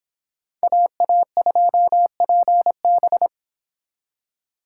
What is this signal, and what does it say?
Morse code, heard as a keyed tone, about 25 wpm, 700 Hz. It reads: AA2PB